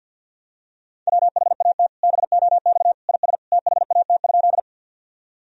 Morse code, heard as a keyed tone, about 52 wpm, 700 Hz. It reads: WHAT BOX IS THAT?
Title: Sentences from Top 1000 Words - Review Part 2 - 50wpm